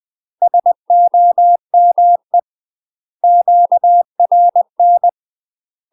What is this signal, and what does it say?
Morse code, heard as a keyed tone, about 20 wpm, 700 Hz. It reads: SOME QRN
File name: UK MASTER QSOs TRAINING/20WPM